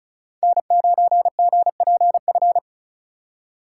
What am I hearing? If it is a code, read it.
Morse code, heard as a keyed tone, about 36 wpm, 700 Hz. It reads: N9GPF
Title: USA Call Signs - 4x Character Spacing - 35wpm